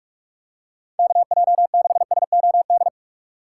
Morse code, heard as a keyed tone, about 46 wpm, 700 Hz. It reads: KJ6SOD
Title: USA Call Signs - 6x Character Spacing - 45wpm